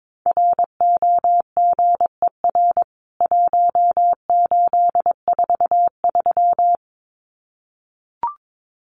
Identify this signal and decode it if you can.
Morse code, heard as a keyed tone, about 22 wpm, 700 Hz. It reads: ROGER 1843
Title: CWT Exchange 22wpm